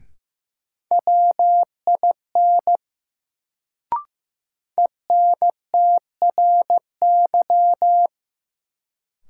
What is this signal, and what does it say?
Morse code, heard as a keyed tone, about 15 wpm, 700 Hz. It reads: WIN ENTRY
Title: Sentences from Top 1500 Words - New Word Drill - 15wpm